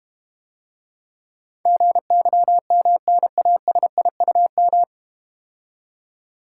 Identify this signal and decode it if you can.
Morse code, heard as a keyed tone, about 32 wpm, 690 Hz. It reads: GYMNASIUM